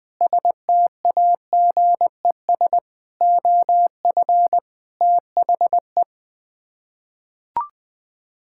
Morse code, heard as a keyed tone, about 21 wpm, 700 Hz. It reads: STAGES OF THE
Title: Sets of 6 Words - 20wpm